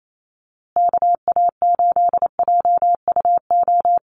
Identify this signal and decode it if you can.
Morse code, heard as a keyed tone, about 28 wpm, 700 Hz. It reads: KA8JUO
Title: USA Call Signs Repeated 2x 28wpm